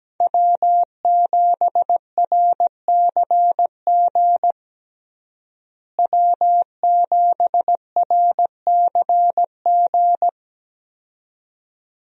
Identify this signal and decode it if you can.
Morse code, heard as a keyed tone, about 17 wpm, 700 Hz. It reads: W7RCG W7RCG
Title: USA Call Signs - 5 Characters - Repeated 2x - 17wpm